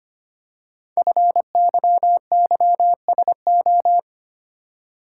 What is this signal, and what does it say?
Morse code, heard as a keyed tone, about 25 wpm, 700 Hz. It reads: FYYSO
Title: Code Groups - 5 Letters - 25wpm